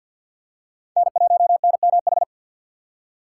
Morse code, heard as a keyed tone, about 51 wpm, 700 Hz. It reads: N1NMH
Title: USA Call Signs - 5 Characters - 2x Character Spacing - 50wpm